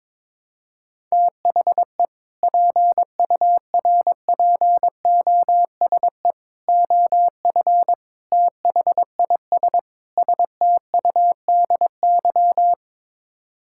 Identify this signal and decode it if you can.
Morse code, heard as a keyed tone, about 22 wpm, 700 Hz. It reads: THE PURPOSE OF THIS STUDY
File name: practice / 22wpm / Build A Phrase - 5-Word Phrases - 22wpm